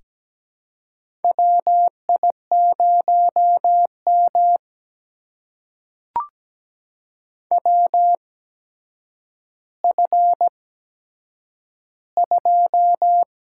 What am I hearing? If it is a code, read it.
Morse code, heard as a keyed tone, about 17 wpm, spaced as for 8 wpm, 700 Hz. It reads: WI0M WF2
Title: USA Call Signs - 4 Characters - 8x Character Spacing - 17wpm